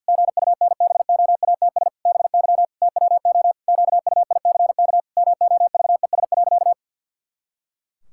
Morse code, heard as a keyed tone, about 53 wpm, 700 Hz. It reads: GUNBOATS BY TWO QUICK MOVES.
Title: Pangram Challenge v2 - 50wpm